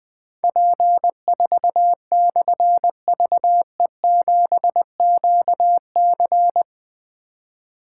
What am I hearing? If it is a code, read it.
Morse code, heard as a keyed tone, about 20 wpm, 700 Hz. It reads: P4/VE7QC
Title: Difficult Call Signs 20wpm